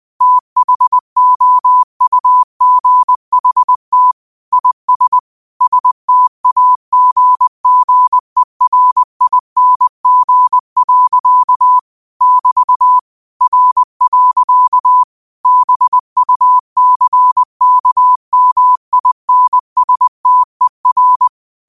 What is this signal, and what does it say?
Morse code, heard as a keyed tone, about 20 wpm, 1000 Hz. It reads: THOUGHT IS STAGGERING. <BT> R. BUCKMINSTER